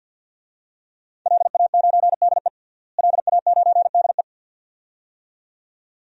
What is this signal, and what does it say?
Morse code, heard as a keyed tone, about 49 wpm, 700 Hz. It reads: LA9DE LA9DE
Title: DXCC-100 Call Signs - 5 Characters - Repeated 2x - 50wpm